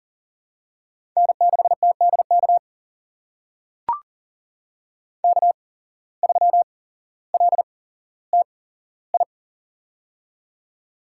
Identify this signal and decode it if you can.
Morse code, heard as a keyed tone, about 37 wpm, spaced as for 17 wpm, 700 Hz. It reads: N6TDK K3LTI
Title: USA Call Signs - 5 Characters - 8x Character Spacing - 40wpm